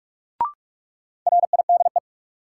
Morse code, heard as a keyed tone, about 43 wpm, 700 Hz. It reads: RIDE
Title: Sentences from Top 1000 Words - Review Part 6 - New Word Drill - 45wpm